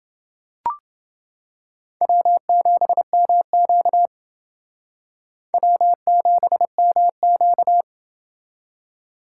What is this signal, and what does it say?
Morse code, heard as a keyed tone, about 29 wpm, 700 Hz. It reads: W7MQ W7MQ